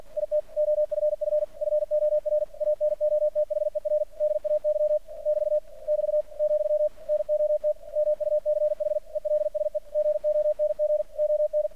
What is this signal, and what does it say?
Morse code, heard as a keyed tone, about 48 wpm, 600 Hz. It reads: IT OFF FOR ANOTHER DAY <BT> <BT> , NOT MUCH ELSE GONG ON
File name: archived / CRSnet